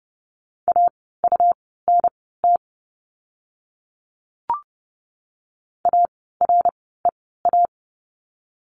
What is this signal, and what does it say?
Morse code, heard as a keyed tone, about 30 wpm, spaced as for 17 wpm, 700 Hz. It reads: AUNT AREA
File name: Lesson 17f - Letter u - Words 30wpm